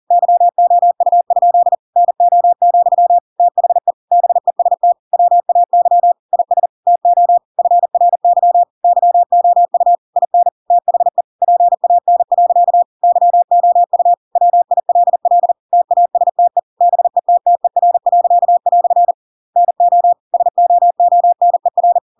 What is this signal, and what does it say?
Morse code, heard as a keyed tone, about 40 wpm, 700 Hz. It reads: YOU? NO, THE BEST WAY IS TO FRY YOU IN THE PAN. YOU WILL TASTE BETTER." NO SOONER
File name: Pinocchio / Pinocchio ch5 40 40 ik1hge